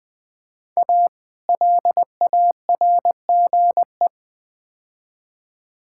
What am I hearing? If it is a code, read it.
Morse code, heard as a keyed tone, about 21 wpm, 700 Hz. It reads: A LARGE
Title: Build A Phrase - 4-Word Phrases - 20wpm